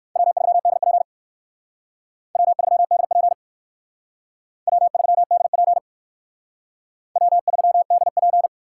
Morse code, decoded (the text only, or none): W3DP W3DP W3DP W3DP